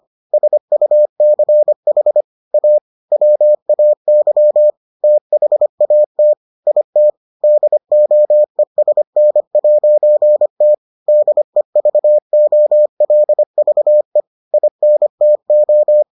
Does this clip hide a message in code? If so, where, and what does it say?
Yes: Morse code from 0.3 to 16.0 s, heard as a keyed tone, about 25 wpm, 600 Hz. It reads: SUCH A WAY THAT IT DOESN'T DEVOLVE INTO